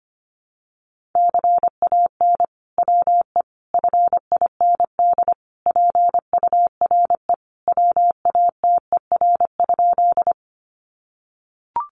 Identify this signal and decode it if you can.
Morse code, heard as a keyed tone, about 25 wpm, 700 Hz. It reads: CAN WE FIND PURE WATER?
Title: Sentences from Top 1500 Words 25wpm